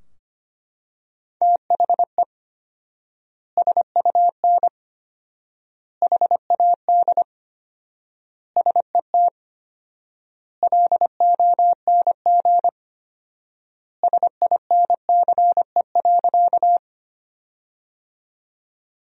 Morse code, heard as a keyed tone, about 25 wpm, 700 Hz. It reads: THE SUN HAD SET LONG SINCE.
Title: 4x Word-Spacing - Sentences from Top 300 Words - 25wpm